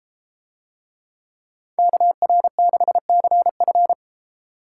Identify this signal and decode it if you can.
Morse code, heard as a keyed tone, about 30 wpm, 700 Hz. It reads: KR6CF